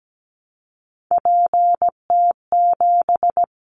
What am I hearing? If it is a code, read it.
Morse code, heard as a keyed tone, about 17 wpm, 700 Hz. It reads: PT7
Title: Callsign Prefixes for Top 100 DXCC - 17wpm